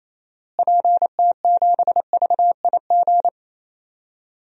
Morse code, heard as a keyed tone, about 28 wpm, 700 Hz. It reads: PT7VIG
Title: DXCC-100 Call Signs - 6 Characters - Repeated 3x - 28wpm